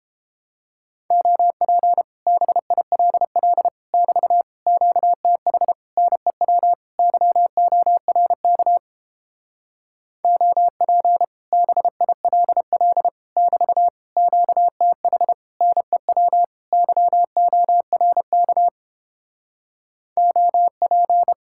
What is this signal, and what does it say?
Morse code, heard as a keyed tone, about 29 wpm, 700 Hz. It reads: OP BILL <BT> QTH NEW YORK OP BILL <BT> QTH NEW YORK OP